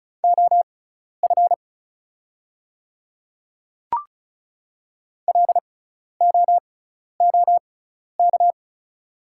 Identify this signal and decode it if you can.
Morse code, heard as a keyed tone, about 34 wpm, spaced as for 12 wpm, 690 Hz. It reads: OF LOOK
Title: 6x Character Spacing - Top 100 Words - 35wpm